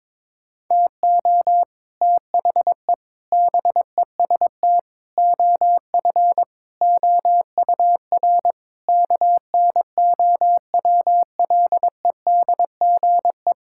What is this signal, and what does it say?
Morse code, heard as a keyed tone, about 22 wpm, 700 Hz. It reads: TO THE BEST OF OUR KNOWLEDGE
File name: Sets of 9 Words - 22wpm